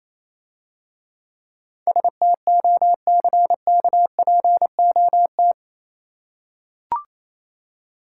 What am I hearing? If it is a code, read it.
Morse code, heard as a keyed tone, about 27 wpm, 700 Hz. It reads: STOCKPOT
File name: Easy Words - Speed-Racing with Hints - 22wpm